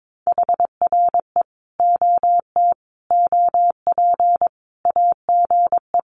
Morse code, heard as a keyed tone, about 22 wpm, 700 Hz. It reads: HRE OT OP AGE
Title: Common QSO Phrases 22wpm